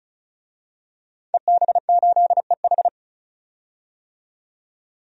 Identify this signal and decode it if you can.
Morse code, heard as a keyed tone, about 36 wpm, 700 Hz. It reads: EB8EH
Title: DXCC-100 Call Signs - 5 Characters - 35wpm